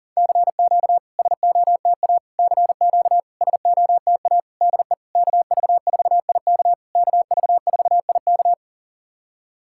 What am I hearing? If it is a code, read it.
Morse code, heard as a keyed tone, about 40 wpm, 700 Hz. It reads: CQ SOTA CQ SOTA DE KV4IK KV4IK
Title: SOTA QSO Practice 40wpm